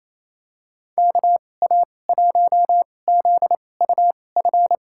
Morse code, heard as a keyed tone, about 27 wpm, spaced as for 20 wpm, 700 Hz. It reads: KA1ZUF
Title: USA Call Signs - 6 Characters - 2x Character Spacing - 28wpm